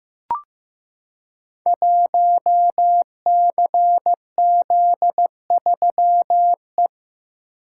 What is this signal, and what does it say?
Morse code, heard as a keyed tone, about 15 wpm, 700 Hz. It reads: 1CZ3E
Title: Code Groups - 5 Letter:Numbers - 15wpm